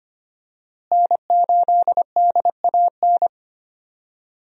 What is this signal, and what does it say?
Morse code, heard as a keyed tone, about 25 wpm, 700 Hz. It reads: N8DAN
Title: USA Call Signs - 5 Characters - Repeated 2x - 25wpm